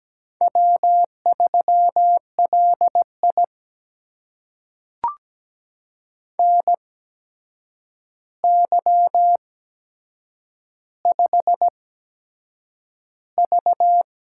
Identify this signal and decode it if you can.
Morse code, heard as a keyed tone, about 17 wpm, spaced as for 7 wpm, 700 Hz. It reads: W3LI NY5V
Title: USA Call Signs - 4 Characters - 8x Character Spacing - 17wpm